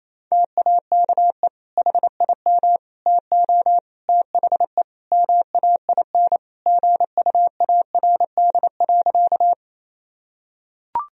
Morse code, heard as a keyed tone, about 28 wpm, 700 Hz. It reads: TAKE HIM TO THE MAIN GUARD.
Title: Sentences from Top 1000 Words - Review Part 4 - 28wpm